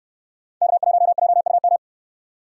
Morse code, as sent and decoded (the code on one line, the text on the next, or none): -.- .---- .--- ..- --
K1JUM